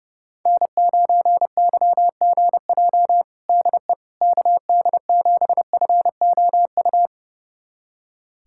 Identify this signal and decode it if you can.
Morse code, heard as a keyed tone, about 32 wpm, 700 Hz. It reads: N9YGJ DE KD7FOU
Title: USA Callsign DE Callsign 30wpm